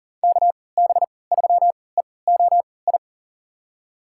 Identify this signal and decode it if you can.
Morse code, heard as a keyed tone, about 40 wpm, spaced as for 23 wpm, 700 Hz. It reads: KB3EOI
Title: USA Call Signs - 6 Characters - 3x Character Spacing - 40wpm